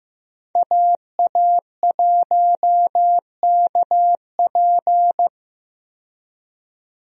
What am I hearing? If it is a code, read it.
Morse code, heard as a keyed tone, about 15 wpm, 700 Hz. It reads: AA1KP
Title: USA Call Signs - 5 Characters - 15wpm